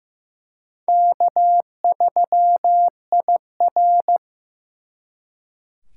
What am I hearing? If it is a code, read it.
Morse code, heard as a keyed tone, about 18 wpm, 700 Hz. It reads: K3IR
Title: Speed Racing - USA 1x2 Call Signs 15wpm